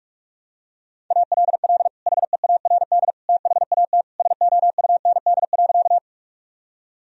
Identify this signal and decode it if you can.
Morse code, heard as a keyed tone, about 41 wpm, 700 Hz. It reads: ALL HEARD THAT SOUND.
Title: Sentences from Top 1000 Words - Review Part 1 - 45wpm